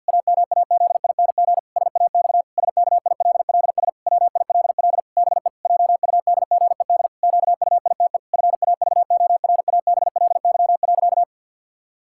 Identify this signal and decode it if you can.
Morse code, heard as a keyed tone, about 50 wpm, 700 Hz. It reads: AMAZING SAX SKILLS WILL BE JUDGED QUITE FAVORABLY.
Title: Pangram Challenge v2 - 50wpm